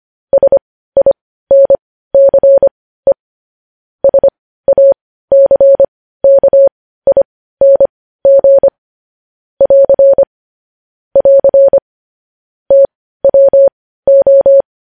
Morse code, heard as a keyed tone, about 25 wpm, spaced as for 15 wpm, 550 Hz. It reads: SINCE SACKING <AR> <AR> TWO
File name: podcast / News headlines-1x-25x15